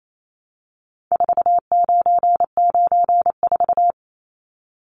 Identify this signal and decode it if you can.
Morse code, heard as a keyed tone, about 28 wpm, 700 Hz. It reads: 4994
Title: Numbers - 4-Digit 28wpm